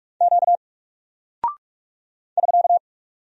Morse code, heard as a keyed tone, about 44 wpm, 700 Hz. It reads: Q <SK>
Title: Single Character (Letter:Number:Punctuation:Prosign) 45wpm